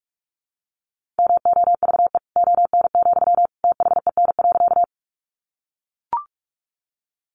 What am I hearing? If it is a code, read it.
Morse code, heard as a keyed tone, about 44 wpm, 700 Hz. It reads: MOVE ON, THEN.
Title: Sentences from Top 200 Words 45wpm